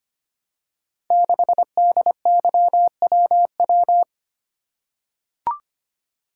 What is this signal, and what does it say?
Morse code, heard as a keyed tone, about 25 wpm, 700 Hz. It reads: 6DYWW
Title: Code Groups - 5 Letter:Numbers - 25wpm